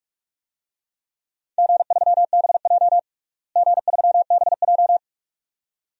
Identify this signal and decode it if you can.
Morse code, heard as a keyed tone, about 46 wpm, 700 Hz. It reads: G3BJ G3BJ
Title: DXCC-100 Call Signs - 4 Characters - Repeated 2x - 45wpm